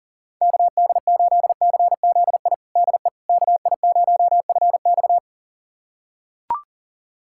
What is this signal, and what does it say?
Morse code, heard as a keyed tone, about 40 wpm, 700 Hz. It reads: KD8CZI DE KI0FX
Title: USA Callsign DE Callsign 40wpm